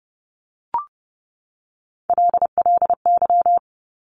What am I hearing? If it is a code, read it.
Morse code, heard as a keyed tone, about 30 wpm, 700 Hz. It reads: LLY
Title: Top Three Letter Combinations in Words 30wpm